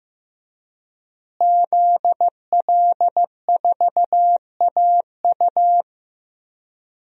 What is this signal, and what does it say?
Morse code, heard as a keyed tone, about 15 wpm, 700 Hz. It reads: ZL4AU
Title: DXCC-100 Call Signs - 5 Characters - Repeated 2x - 15wpm